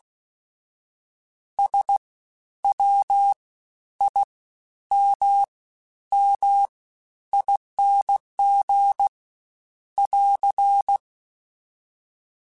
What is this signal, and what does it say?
Morse code, heard as a keyed tone, about 16 wpm, 800 Hz. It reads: S W I M M ING <AR>
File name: Special INGs-mp3s / Special INGs-mp3s / Fast ing @15